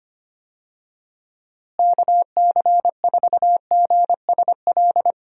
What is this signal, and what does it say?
Morse code, heard as a keyed tone, about 25 wpm, 700 Hz. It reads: KC4GSL
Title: USA Call Signs - 6 Characters - Repeated 2x - 25wpm